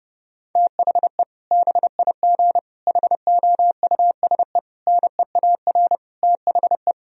Morse code, heard as a keyed tone, about 30 wpm, 700 Hz. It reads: THE BIG HOUSE NEAR THE
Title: Sentences from Top 1000 Words - Review Part 1 - 30wpm